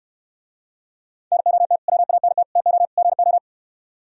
Morse code, heard as a keyed tone, about 68 wpm, 700 Hz. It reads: NOT WANT TO GO